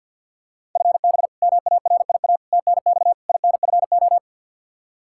Morse code, heard as a keyed tone, about 52 wpm, 700 Hz. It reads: FB MARIA TNX INFO
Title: Practice QSO - Exchange 3 - More Info - 50wpm